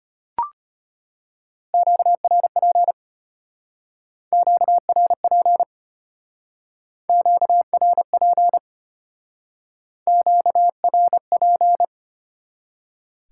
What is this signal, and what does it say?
Morse code, heard as a keyed tone, about 32 wpm, 700 Hz. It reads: QRP QRP QRP QRP